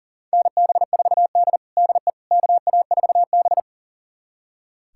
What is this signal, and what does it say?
Morse code, heard as a keyed tone, about 40 wpm, 700 Hz. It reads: NB4D DE KA4B
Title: USA Callsign DE Callsign - 4 Characters - 40wpm